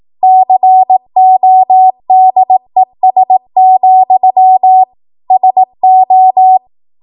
Morse code, heard as a keyed tone, about 18 wpm, 750 Hz. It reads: CODES, SO
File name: Morse / 18WPM